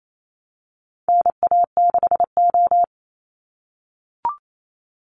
Morse code, heard as a keyed tone, about 28 wpm, 700 Hz. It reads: NA6O
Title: Call Signs - 28wpm